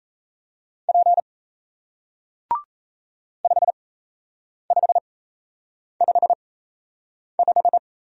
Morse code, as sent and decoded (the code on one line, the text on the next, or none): .--. / ..... ..... ..... .....
P 5555